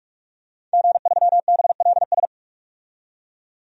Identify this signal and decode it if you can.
Morse code, heard as a keyed tone, about 46 wpm, 700 Hz. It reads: G3BLS
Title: DXCC-100 Call Signs - 5 Characters - 45wpm